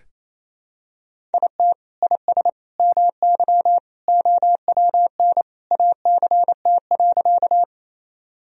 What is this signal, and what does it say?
Morse code, heard as a keyed tone, about 28 wpm, 700 Hz. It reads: IT IS MY OWN ACT.
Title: Sentences from Top 1000 Words - Review Part 1 - 28wpm